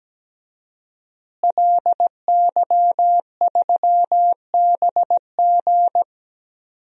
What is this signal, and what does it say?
Morse code, heard as a keyed tone, about 17 wpm, 700 Hz. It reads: LY3BG
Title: DXCC-100 Call Signs - 5 Characters - Repeated 2x - 17wpm